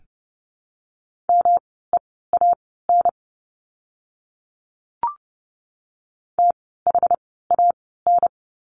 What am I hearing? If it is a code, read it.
Morse code, heard as a keyed tone, about 30 wpm, spaced as for 17 wpm, 700 Hz. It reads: MEAN THAN
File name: course / 30:10wpm / 230 Lesson 37f - Letter x - Words 30wpm